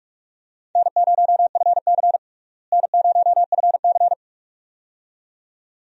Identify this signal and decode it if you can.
Morse code, heard as a keyed tone, about 46 wpm, 700 Hz. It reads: N0FC N0FC
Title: USA Call Signs - 4 Characters - Repeated 2x - 45wpm